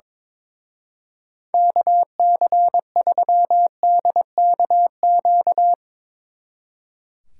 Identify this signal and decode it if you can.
Morse code, heard as a keyed tone, about 22 wpm, 700 Hz. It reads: KC3DKQ
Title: Speed Racing - USA 2x3 Call Signs 22wpm